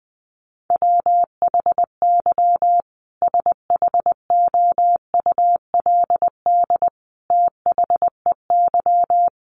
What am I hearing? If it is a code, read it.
Morse code, heard as a keyed tone, about 20 wpm, 700 Hz. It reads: WHY SHOULD THEY
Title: Sentences from Top 1100 Words 20wpm